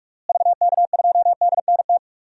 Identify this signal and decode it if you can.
Morse code, heard as a keyed tone, about 43 wpm, 700 Hz. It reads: VK2DNT
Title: DXCC-100 Call Signs - 6 Characters - Repeated 3x - 45wpm